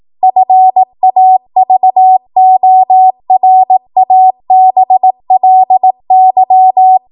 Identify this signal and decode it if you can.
Morse code, heard as a keyed tone, about 18 wpm, 750 Hz. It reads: FAVORABLY
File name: Morse / 18WPM